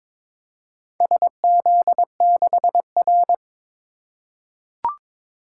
Morse code, heard as a keyed tone, about 22 wpm, 700 Hz. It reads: SZ6R
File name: practice / 22wpm / Four Letter-Number Combinations - 22wpm